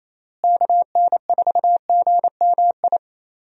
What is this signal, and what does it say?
Morse code, heard as a keyed tone, about 28 wpm, 700 Hz. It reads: KN4GMI
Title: USA Call Signs - 6 Characters - Repeated 3x - 28wpm